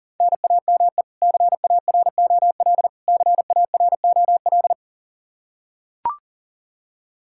Morse code, heard as a keyed tone, about 40 wpm, 700 Hz. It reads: NAME CAROL CAROL
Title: Practice QSO - Exchange 1 - Beginning - 40wpm